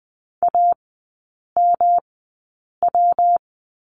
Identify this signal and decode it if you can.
Morse code, heard as a keyed tone, about 20 wpm, spaced as for 8 wpm, 700 Hz. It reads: AMW
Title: 114 Lesson 19e - Letter w - Three Characters 20wpm